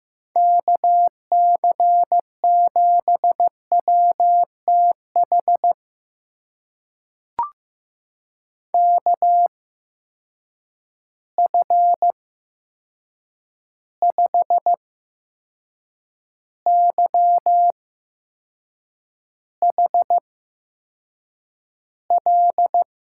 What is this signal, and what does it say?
Morse code, heard as a keyed tone, about 15 wpm, spaced as for 6 wpm, 700 Hz. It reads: KC7WTH KF5YHL
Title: USA Call Signs - 6 Characters - 8x Character Spacing - 15wpm